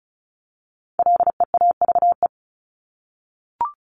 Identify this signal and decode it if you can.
Morse code, heard as a keyed tone, about 34 wpm, 700 Hz. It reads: LEAVE
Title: ING Suffix - Top 100 Words with Base Words 35wpm